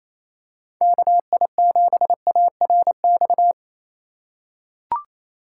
Morse code, heard as a keyed tone, about 28 wpm, 700 Hz. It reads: KI7ARX
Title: USA Call Signs - 6 Characters - 2x Character Spacing - 28wpm